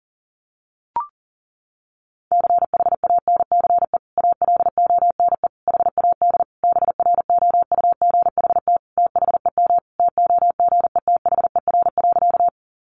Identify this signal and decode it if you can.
Morse code, heard as a keyed tone, about 40 wpm, 700 Hz. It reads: CHANCE ALONE HAD BROUGHT THEM TOGETHER.
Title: Sentences from Top 1000 Words 40wpm